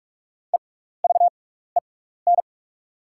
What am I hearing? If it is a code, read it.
Morse code, heard as a keyed tone, about 44 wpm, spaced as for 15 wpm, 690 Hz. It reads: EVEN
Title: 6x Character Spacing - Top 200 Words - 45wpm